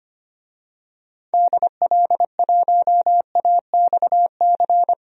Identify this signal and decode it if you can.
Morse code, heard as a keyed tone, about 25 wpm, 700 Hz. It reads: DL1AXC
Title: DXCC-100 Call Signs - 6 Characters - 25wpm